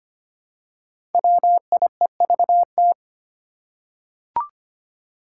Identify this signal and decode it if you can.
Morse code, heard as a keyed tone, about 25 wpm, 700 Hz. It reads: WIEVT
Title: Five Challenging Letter-Number Combinations 25wpm